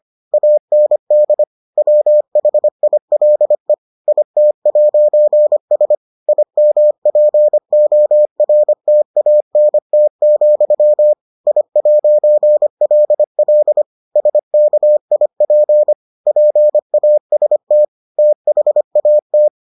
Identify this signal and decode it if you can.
Morse code, heard as a keyed tone, about 25 wpm, 600 Hz. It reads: AND WHILE IT'S IMPORTANT, I'LL SKIP PAST THAT